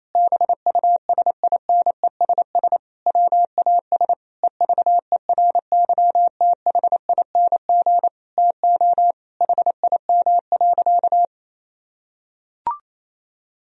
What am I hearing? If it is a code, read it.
Morse code, heard as a keyed tone, about 28 wpm, 700 Hz. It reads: BUSINESS WAS EVERYTHING TO HIM.
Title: Sentences from Top 1000 Words - Review Part 3 - 28wpm